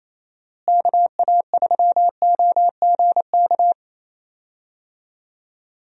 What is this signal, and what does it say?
Morse code, heard as a keyed tone, about 28 wpm, 700 Hz. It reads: KA3OGK
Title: USA Call Signs - 6 Characters - Repeated 3x - 28wpm